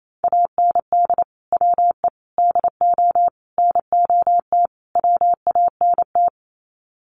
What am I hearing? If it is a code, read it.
Morse code, heard as a keyed tone, about 28 wpm, 700 Hz. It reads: AND WE DO NOT WANT
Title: Sets of 5 Words Encore2 28wpm